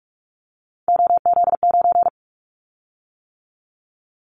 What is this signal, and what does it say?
Morse code, heard as a keyed tone, about 44 wpm, 700 Hz. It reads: OZ9
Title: Callsign Prefixes for Top 50 DXCC - 45wpm